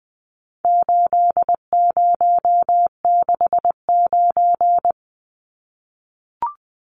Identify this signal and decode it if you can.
Morse code, heard as a keyed tone, about 20 wpm, 700 Hz. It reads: 8069
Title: Numbers - 4-Digit 20wpm